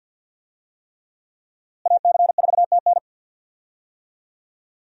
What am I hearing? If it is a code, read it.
Morse code, heard as a keyed tone, about 50 wpm, 700 Hz. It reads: AC4TN
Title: USA Call Signs - 5 Characters - Repeated 3x - 50wpm